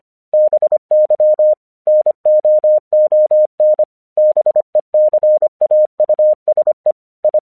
Morse code, heard as a keyed tone, about 25 wpm, 610 Hz. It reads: BY NOON BECAUSE I
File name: 1 English 25wpm / AQC36